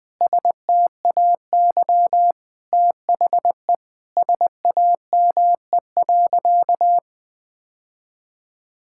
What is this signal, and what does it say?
Morse code, heard as a keyed tone, about 20 wpm, 700 Hz. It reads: STAY THE SAME.
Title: Sentences with 7-Word Phrases - 20wpm